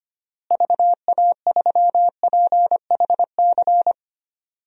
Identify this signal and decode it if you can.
Morse code, heard as a keyed tone, about 25 wpm, 700 Hz. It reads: VA3PHC
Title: DXCC-100 Call Signs - 6 Characters - Repeated 3x - 25wpm